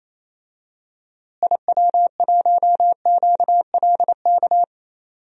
Morse code, heard as a keyed tone, about 28 wpm, 700 Hz. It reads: IW1QLK
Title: DXCC-100 Call Signs - 6 Characters - Repeated 2x - 28wpm